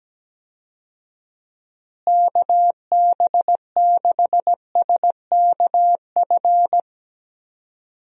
Morse code, heard as a keyed tone, about 17 wpm, 700 Hz. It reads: KB6SKF
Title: USA Call Signs - 6 Characters - Repeated 2x - 17wpm